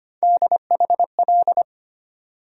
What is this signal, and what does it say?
Morse code, heard as a keyed tone, about 25 wpm, 700 Hz. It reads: DHL
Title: USA Call Sign Postfixes - 3 Letters 25wpm